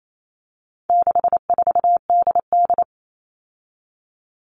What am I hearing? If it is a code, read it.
Morse code, heard as a keyed tone, about 28 wpm, 700 Hz. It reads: 64DD
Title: All Cut Numbers - 4-Digit 28wpm